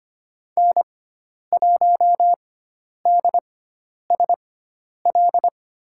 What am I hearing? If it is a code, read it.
Morse code, heard as a keyed tone, about 25 wpm, spaced as for 10 wpm, 700 Hz. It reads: N1DSL
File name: Lesson 16g - Letter c - Call Signs 25wpm